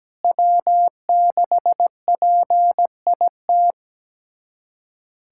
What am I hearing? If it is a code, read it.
Morse code, heard as a keyed tone, about 17 wpm, 700 Hz. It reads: W6PIT
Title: USA Callsign DE Callsign - 5 Characters - 17wpm